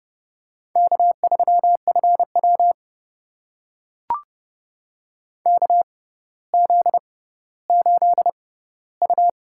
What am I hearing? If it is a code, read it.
Morse code, heard as a keyed tone, about 31 wpm, spaced as for 15 wpm, 700 Hz. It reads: K3FW KZ8U
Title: USA Call Signs - 4 Characters - 6x Character Spacing - 30wpm